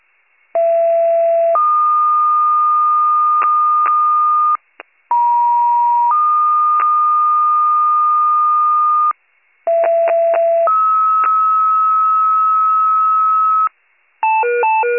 Two-tone page, heard as a one-second tone, then a three-second tone, about 670 then 1220 Hz, about 950 then 1220 Hz, about 670 then 1290 Hz. It is heard from 0.5 to 13.7 s.